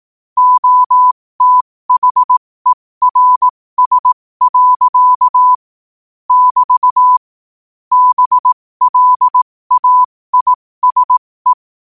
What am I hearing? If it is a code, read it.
Morse code, heard as a keyed tone, about 18 wpm, spaced as for 15 wpm, 1000 Hz. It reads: OTHERS. <BT> BLAISE
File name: qotdfiles / 15wpm